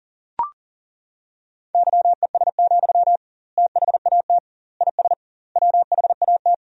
Morse code, heard as a keyed tone, about 40 wpm, 700 Hz. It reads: YES, THAT IS WHAT
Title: 2x Word-Spacing - Sentences from Top 500 Words - 40wpm